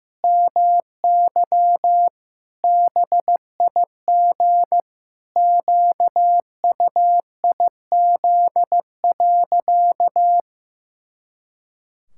Morse code, heard as a keyed tone, about 15 wpm, 700 Hz. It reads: MY BIG QUIZ.
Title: Pangram Challenge v2 - 15wpm